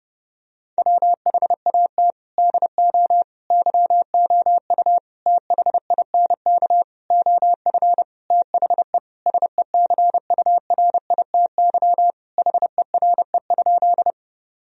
Morse code, heard as a keyed tone, about 32 wpm, 700 Hz. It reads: WHAT DO YOU THINK OF THE SECURITY HERE?
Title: Sentences from Top 1000 Words - Review Part 4 - 30wpm